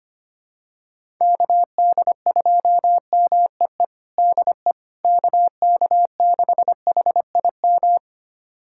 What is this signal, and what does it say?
Morse code, heard as a keyed tone, about 25 wpm, 700 Hz. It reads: KD2MEE DE KK6HIM
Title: USA Callsign DE Callsign - 6 Characters - 25wpm